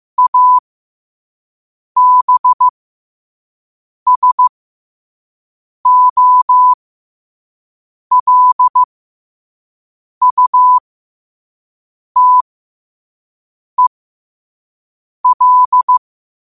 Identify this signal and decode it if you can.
Morse code, heard as a keyed tone, about 15 wpm, spaced as for 5 wpm, 1000 Hz. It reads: ABSOLUTEL
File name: qotdfiles / 5wpm